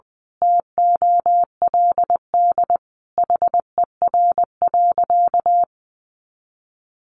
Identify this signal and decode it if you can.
Morse code, heard as a keyed tone, about 20 wpm, 700 Hz. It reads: TOLD HER.
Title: Sentences from Top 1200 Words 20wpm